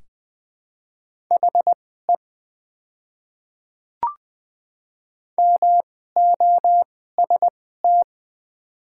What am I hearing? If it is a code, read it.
Morse code, heard as a keyed tone, about 20 wpm, spaced as for 14 wpm, 700 Hz. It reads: HE MOST
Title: 2x Character Spacing - Top 100 Words - 20wpm